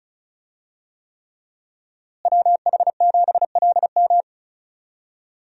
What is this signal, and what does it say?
Morse code, heard as a keyed tone, about 34 wpm, 700 Hz. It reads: WH7LM
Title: USA Call Signs - 5 Characters - Repeated 3x - 35wpm